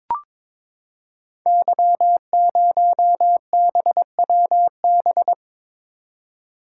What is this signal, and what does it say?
Morse code, heard as a keyed tone, about 23 wpm, 700 Hz. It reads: Y0BWB
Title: Code Groups - 5 Letter:Numbers - 22wpm